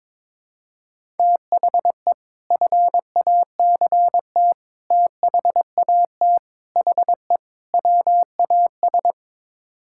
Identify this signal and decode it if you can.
Morse code, heard as a keyed tone, about 22 wpm, 700 Hz. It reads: THE FACT THAT HE WAS
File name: practice / 22wpm / Build A Phrase - 5-Word Phrases - 22wpm